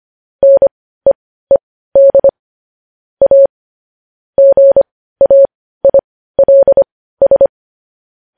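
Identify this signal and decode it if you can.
Morse code, heard as a keyed tone, about 25 wpm, spaced as for 15 wpm, 550 Hz. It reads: NEED A GAILS
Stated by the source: News headlines in Morse code at 15 words per minute, characters sent at 25 WPM.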